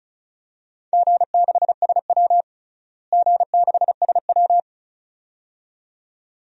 Morse code, heard as a keyed tone, about 36 wpm, 700 Hz. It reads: G6SW G6SW
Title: DXCC-100 Call Signs - 4 Characters - Repeated 2x - 35wpm